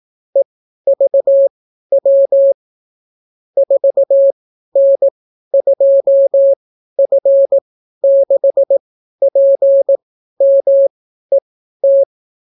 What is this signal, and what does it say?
Morse code, heard as a keyed tone, about 18 wpm, spaced as for 12 wpm, 550 Hz. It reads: EVW 4N2F6PMET